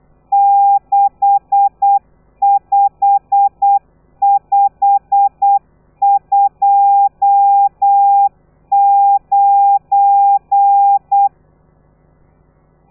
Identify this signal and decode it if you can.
Morse code, heard as a keyed tone, about 8 wpm, 800 Hz. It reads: 65529
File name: telegrafia / i2tfj / Lez12 35